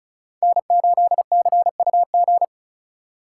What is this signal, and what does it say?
Morse code, heard as a keyed tone, about 36 wpm, 700 Hz. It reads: N8CUG
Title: USA Call Signs - 5 Characters - 2x Character Spacing - 35wpm